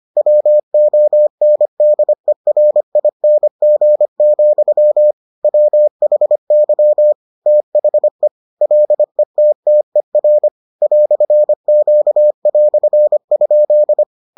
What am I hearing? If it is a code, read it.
Morse code, heard as a keyed tone, about 25 wpm, 600 Hz. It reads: WONDERING, WHY THE LETTER "Q"?